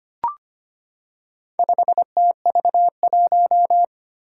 Morse code, heard as a keyed tone, about 25 wpm, 700 Hz. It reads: 5TV1
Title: Four Challenging Letter-Number Combinations 25wpm